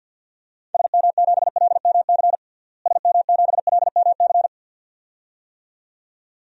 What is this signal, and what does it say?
Morse code, heard as a keyed tone, about 50 wpm, 700 Hz. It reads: SM7LMC SM7LMC
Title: DXCC-100 Call Signs - 6 Characters - Repeated 2x - 50wpm